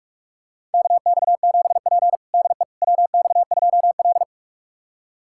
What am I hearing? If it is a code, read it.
Morse code, heard as a keyed tone, about 46 wpm, 700 Hz. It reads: KX7P DE WX2L
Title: USA Callsign DE Callsign - 4 Characters - 45wpm